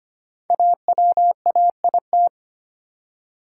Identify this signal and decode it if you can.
Morse code, heard as a keyed tone, about 25 wpm, 700 Hz. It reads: AWAIT
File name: 5 Letter Words - First 3 Letters Given - 25wpm